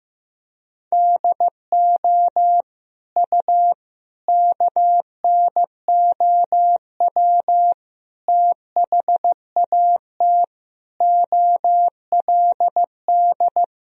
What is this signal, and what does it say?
Morse code, heard as a keyed tone, about 15 wpm, 700 Hz. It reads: DO U KNOW THAT OLD
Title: Sentences from Top 400 Words and Abbreviations 15wpm